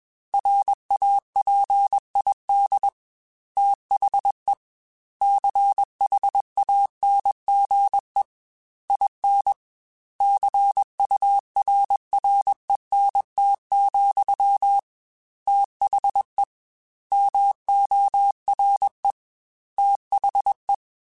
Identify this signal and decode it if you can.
Morse code, heard as a keyed tone, about 21 wpm, 800 Hz. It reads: RAPID THE CHANGE IN CURRENT, THE MORE THE
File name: Technical Subjects / Technical Subjects-mp3s / XL @20